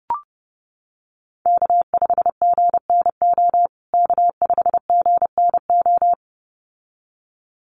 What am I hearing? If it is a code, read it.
Morse code, heard as a keyed tone, about 30 wpm, 700 Hz. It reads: K5GNO K5GNO
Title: USA Call Signs Repeated 2x 30wpm